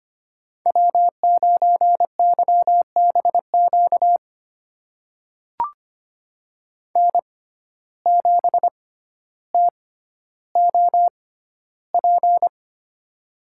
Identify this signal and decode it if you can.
Morse code, heard as a keyed tone, about 25 wpm, spaced as for 13 wpm, 700 Hz. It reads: W9YBQ N7TOP
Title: USA Call Signs - 5 Characters - 6x Character Spacing - 25wpm